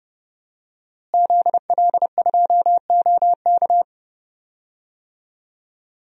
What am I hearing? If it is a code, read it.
Morse code, heard as a keyed tone, about 30 wpm, 700 Hz. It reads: ZL2OK
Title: DXCC-100 Call Signs - 5 Characters - Repeated 3x - 30wpm